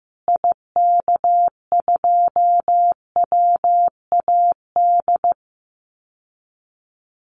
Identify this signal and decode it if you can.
Morse code, heard as a keyed tone, about 15 wpm, 700 Hz. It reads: IK2WAD
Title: Call Signs Repeated 2x - 15wpm